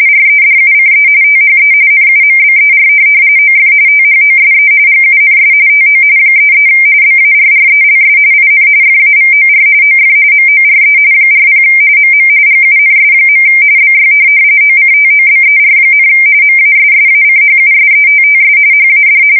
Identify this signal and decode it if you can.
RTTY, about 45.45 baud. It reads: REDUCE THE NUMBER OF THOSE GUNS TO 33 BARRELS IN FAVOR OF MORE 40 MM GUNS. BY OCTOBER, MASSACHUSETTS HAD
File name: fsk170 / USS-Massachusetts